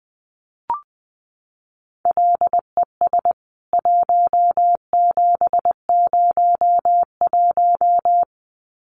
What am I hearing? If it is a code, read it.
Morse code, heard as a keyed tone, about 20 wpm, 700 Hz. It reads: LES 1701
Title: CWT Exchange 20wpm